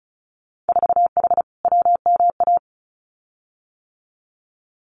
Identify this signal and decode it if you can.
Morse code, heard as a keyed tone, about 34 wpm, 700 Hz. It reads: 4H WMA
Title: Winter Field Day Exchanges Encore 35wpm